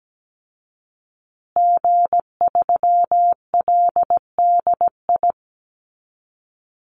Morse code, heard as a keyed tone, about 17 wpm, 700 Hz. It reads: G3LDI
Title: Call Signs Repeated 2x - 17wpm